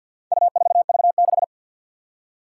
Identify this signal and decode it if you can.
Morse code, heard as a keyed tone, about 45 wpm, 700 Hz. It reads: U4V6
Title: Four Challenging Letter-Number Combinations 50wpm